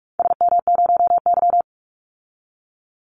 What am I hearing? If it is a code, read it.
Morse code, heard as a keyed tone, about 46 wpm, 700 Hz. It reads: SM0Y
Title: Call Signs - 45wpm